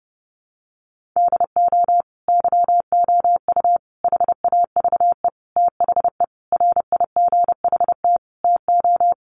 Morse code, heard as a keyed tone, about 30 wpm, 700 Hz. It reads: DO YOU HAVE THE RIGHT TO
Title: Sentences from Top 1400 Words 30wpm